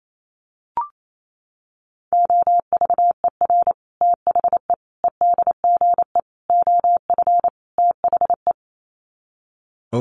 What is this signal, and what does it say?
Morse code, heard as a keyed tone, about 28 wpm, 700 Hz. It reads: OVER THE EDGE OF THE
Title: Sets of 5 Words Encore2 28wpm